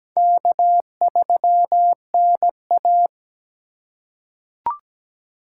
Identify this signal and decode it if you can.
Morse code, heard as a keyed tone, about 17 wpm, 700 Hz. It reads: K3NA
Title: USA Call Signs - 4x Character Spacing - 17wpm